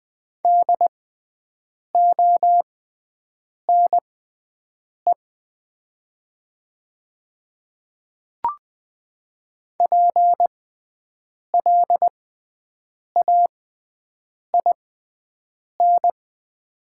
Morse code, heard as a keyed tone, about 20 wpm, spaced as for 7 wpm, 700 Hz. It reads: DONE PLAIN
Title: 6x Character Spacing - Top 500 Words - 20wpm